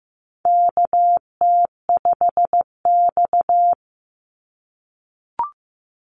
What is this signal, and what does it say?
Morse code, heard as a keyed tone, about 15 wpm, 700 Hz. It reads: KT5X
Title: NA SOTA Activator Callsigns 15wpm